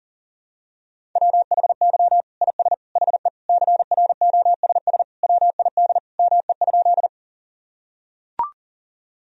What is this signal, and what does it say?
Morse code, heard as a keyed tone, about 40 wpm, 700 Hz. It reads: WHY IS HE CROSS WID ME?
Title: Sentences from Top 300 Words and Abbreviations 40wpm